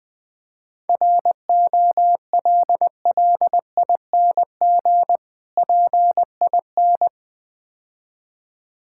Morse code, heard as a keyed tone, about 22 wpm, 690 Hz. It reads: ROLLING PIN
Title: Difficult Words - Speed-Racing with Hints - 20wpm